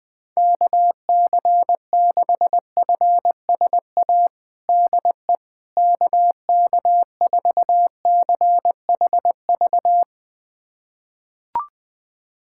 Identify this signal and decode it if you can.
Morse code, heard as a keyed tone, about 20 wpm, 700 Hz. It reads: KC6FSA DE KK4CHV
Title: USA Callsign DE Callsign 20wpm